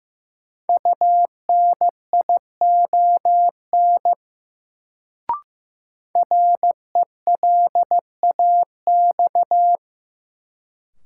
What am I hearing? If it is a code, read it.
Morse code, heard as a keyed tone, about 15 wpm, 700 Hz. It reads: UNION RELAX
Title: Sentences from Top 1400 Words - New Word Drill - 15wpm